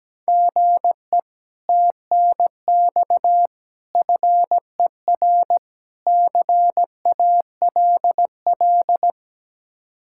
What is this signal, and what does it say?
Morse code, heard as a keyed tone, about 17 wpm, 700 Hz. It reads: GE TNX FER CALL
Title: Speed-Racing Common QSO Phrases - 17wpm